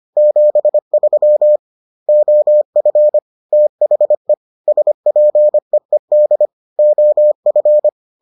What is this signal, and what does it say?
Morse code, heard as a keyed tone, about 25 wpm, 600 Hz. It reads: 73 OF THE SPEED OF